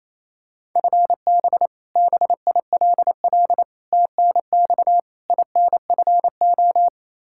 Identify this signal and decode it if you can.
Morse code, heard as a keyed tone, about 28 wpm, 700 Hz. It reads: FB BILL TNX INFO
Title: Practice QSO - Exchange 3 - More Info - 28wpm